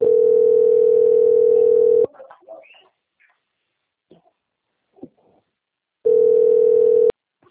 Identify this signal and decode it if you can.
Ringback tone (North America), 440 + 480 Hz.